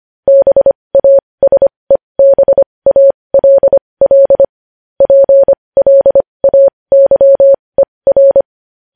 Morse code, heard as a keyed tone, about 25 wpm, spaced as for 20 wpm, 550 Hz. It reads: BASEBALL PLAYER
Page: news headlines in morse code at 20 wpm